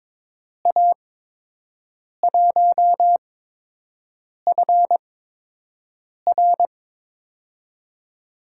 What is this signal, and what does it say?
Morse code, heard as a keyed tone, about 22 wpm, 700 Hz. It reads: A 1 F R
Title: USA Call Signs - 5 Characters - 8x Character Spacing - 22wpm